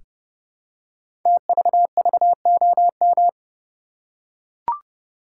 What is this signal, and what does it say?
Morse code, heard as a keyed tone, about 30 wpm, 700 Hz. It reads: TVVOM
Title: Code Groups - 5 Letters - 30wpm